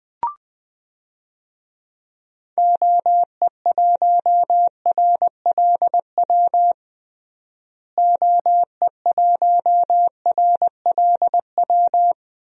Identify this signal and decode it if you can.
Morse code, heard as a keyed tone, about 20 wpm, 700 Hz. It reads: OE1RLW OE1RLW
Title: DXCC-100 Call Signs - 6 Characters - Repeated 3x - 20wpm